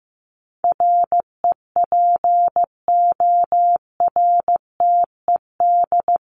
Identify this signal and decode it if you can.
Morse code, heard as a keyed tone, about 14 wpm, 700 Hz. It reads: REPORTED
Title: ED Suffix - Top 100 Words with Base Words 15wpm